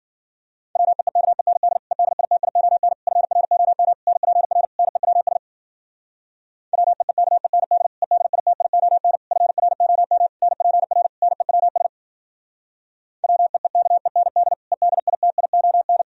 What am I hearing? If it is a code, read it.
Morse code, heard as a keyed tone, about 52 wpm, 700 Hz. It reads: WEEKEND EDITION FROM NPR NEWS WEEKEND EDITION FROM NPR NEWS WEEKEND EDITION